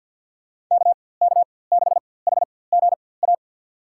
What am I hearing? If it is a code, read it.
Morse code, heard as a keyed tone, about 49 wpm, spaced as for 23 wpm, 700 Hz. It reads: KK6HGA
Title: USA Call Signs - 6 Characters - 4x Character Spacing - 50wpm